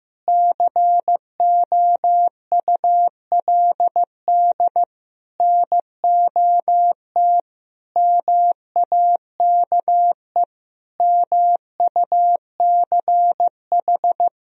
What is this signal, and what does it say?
Morse code, heard as a keyed tone, about 16 wpm, 700 Hz. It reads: COULD NOT MAKE MUCH
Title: Sentences from Top 1000 Words - Review Part 4 - 15wpm